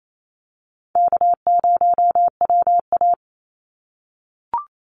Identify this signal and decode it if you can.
Morse code, heard as a keyed tone, about 28 wpm, 700 Hz. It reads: K0WA
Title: Call Signs - 28wpm